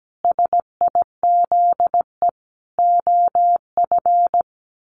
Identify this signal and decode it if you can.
Morse code, heard as a keyed tone, about 17 wpm, 700 Hz. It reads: SIZE OF
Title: Sentences from Top 1400 Words 17wpm